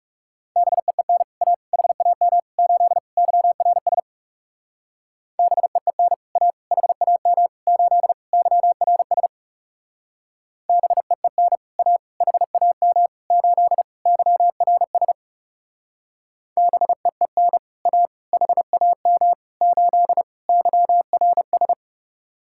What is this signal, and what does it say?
Morse code, heard as a keyed tone, about 39 wpm, 700 Hz. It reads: BEEN A HAM 8 YRS BEEN A HAM 8 YRS BEEN A HAM 8 YRS BEEN A HAM 8 YRS